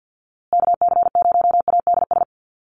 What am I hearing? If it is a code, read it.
Morse code, heard as a keyed tone, about 49 wpm, 700 Hz. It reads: KC0ADS
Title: USA Call Signs Repeated 2x 50wpm